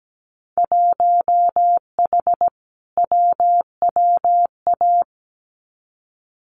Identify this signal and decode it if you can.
Morse code, heard as a keyed tone, about 17 wpm, 700 Hz. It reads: 1H WWA
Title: Winter Field Day Exchanges 17wpm